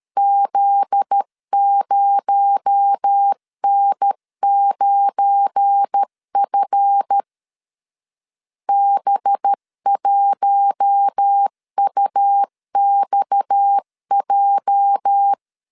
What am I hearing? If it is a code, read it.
Morse code, heard as a keyed tone, about 13 wpm, 800 Hz. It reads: Z0N9F B1UXJ
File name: LEZIONI / 9 286 12wpm 2305